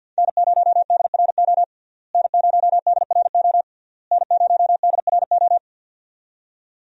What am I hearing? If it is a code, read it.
Morse code, heard as a keyed tone, about 47 wpm, 700 Hz. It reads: N0DRO N0DRO N0DRO
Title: USA Call Signs - 5 Characters - Repeated 3x - 50wpm